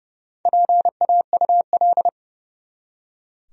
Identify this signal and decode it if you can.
Morse code, heard as a keyed tone, about 30 wpm, 700 Hz. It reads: PAUL
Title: USA Common Names - Max 4 Letters - 30wpm